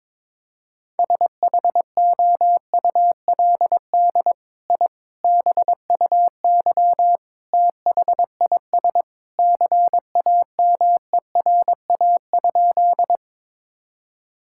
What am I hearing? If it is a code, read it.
Morse code, heard as a keyed tone, about 22 wpm, 700 Hz. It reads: SHOULD I BUY THIS CAMERA?
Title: Sentences from Top 1000 Words - Review Part 5 - 22wpm